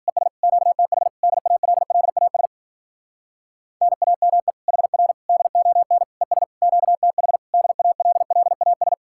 Morse code, heard as a keyed tone, about 47 wpm, 700 Hz. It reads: ES QTH DALLAS NAME HR DON ES QTH DALLAS
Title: Speed-Racing Common QSO Phrases - 40wpm